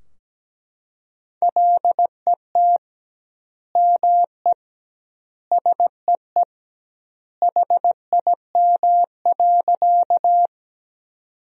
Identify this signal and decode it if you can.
Morse code, heard as a keyed tone, about 17 wpm, 700 Hz. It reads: LET ME SEE HIM.
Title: 2x Word-Spacing - Sentences from Top 300 Words - 17wpm